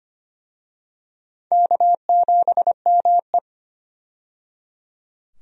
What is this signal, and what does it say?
Morse code, heard as a keyed tone, about 25 wpm, 700 Hz. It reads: K7ME
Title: USA Call Signs - 4 Characters - 25wpm